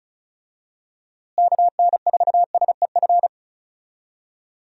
Morse code, heard as a keyed tone, about 36 wpm, 700 Hz. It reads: KN4SEF
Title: USA Call Signs - 6 Characters - Repeated 2x - 35wpm